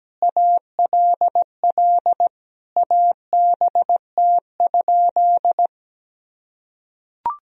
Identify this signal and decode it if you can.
Morse code, heard as a keyed tone, about 17 wpm, 700 Hz. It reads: ALL ABT?
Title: Sentences from Top 100 Words and Abbreviations 17wpm